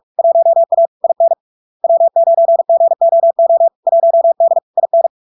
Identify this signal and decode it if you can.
Morse code, heard as a keyed tone, about 46 wpm, 670 Hz. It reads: 1A IN W9GOO 1D IN
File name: LARK-FD-2022-Log-in-CW-45wpm-670-Hz